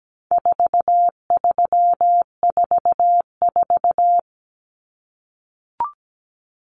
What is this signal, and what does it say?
Morse code, heard as a keyed tone, about 17 wpm, 700 Hz. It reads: 4344
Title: Common Cut Numbers - 4-Digit 17wpm